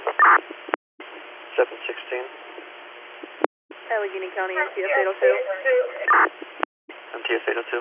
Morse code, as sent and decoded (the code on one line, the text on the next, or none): none